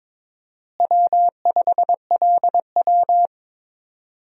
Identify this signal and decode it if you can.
Morse code, heard as a keyed tone, about 22 wpm, 700 Hz. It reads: W5LW
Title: USA Call Signs - 4 Characters - Repeated 3x - 22wpm